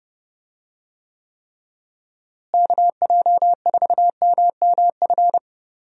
Morse code, heard as a keyed tone, about 30 wpm, 700 Hz. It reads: KJ4MMF
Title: USA Call Signs - 6 Characters - Repeated 3x - 30wpm